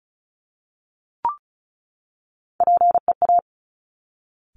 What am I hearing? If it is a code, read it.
Morse code, heard as a keyed tone, about 36 wpm, 700 Hz. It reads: PEA
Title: Top Three Letter Combinations in Words 35wpm